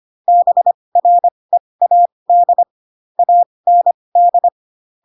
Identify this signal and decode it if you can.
Morse code, heard as a keyed tone, about 25 wpm, spaced as for 20 wpm, 700 Hz. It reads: BREAD AND
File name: Pinocchio / Pinocchio ch2 15 25 ik1hge